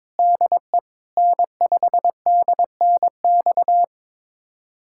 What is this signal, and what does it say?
Morse code, heard as a keyed tone, about 21 wpm, 700 Hz. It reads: DE N5DNX
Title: USA Callsign DE Callsign - 5 Characters - 22wpm